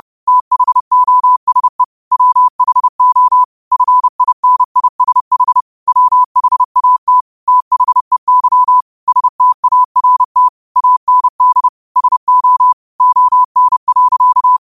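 Morse code, heard as a keyed tone, about 30 wpm, 1000 Hz. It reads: THOSE WHO FINISH WHAT THEY START AND SO ON.
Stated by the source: Morse code quote of the day sent at 30 wpm.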